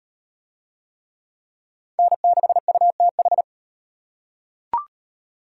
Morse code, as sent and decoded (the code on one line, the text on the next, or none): -. -.... ..- - ....
N6UTH